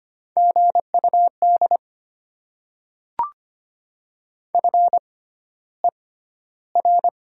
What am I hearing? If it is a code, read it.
Morse code, heard as a keyed tone, about 25 wpm, spaced as for 13 wpm, 700 Hz. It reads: GUD FER
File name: Top 20 QSO Elements - 6x Character Spacing - 25wpm